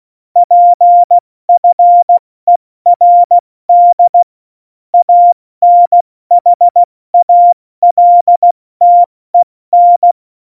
Morse code, heard as a keyed tone, about 16 wpm, 700 Hz. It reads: PFERD ANHALTEN
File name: Morsekurs / Witze / Witz 26 Mai, 2020 70 F80